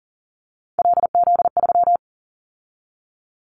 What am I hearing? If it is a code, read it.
Morse code, heard as a keyed tone, about 40 wpm, 700 Hz. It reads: LZ3
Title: Callsign Prefixes for Top 100 DXCC - 40wpm